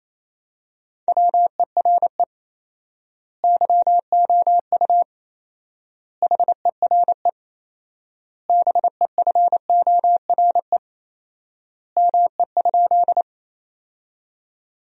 Morse code, heard as a keyed tone, about 28 wpm, 700 Hz. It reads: WERE YOU HERE BEFORE ME?
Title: 4x Word-Spacing - Sentences from Top 200 Words - 28wpm